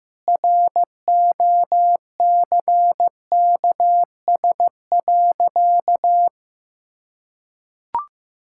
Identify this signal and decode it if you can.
Morse code, heard as a keyed tone, about 15 wpm, 700 Hz. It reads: ROCKS.
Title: Pangram Challenge v2 - 15wpm